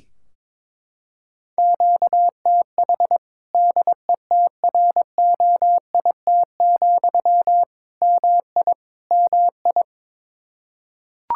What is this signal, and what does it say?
Morse code, heard as a keyed tone, about 21 wpm, 700 Hz. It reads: QTH DETROIT, MI MI
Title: Practice QSO - Exchange 1 - Beginning - 22wpm